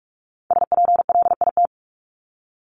Morse code, heard as a keyed tone, about 44 wpm, 700 Hz. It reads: SPLIT
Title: Top 1751-2000 Words 45wpm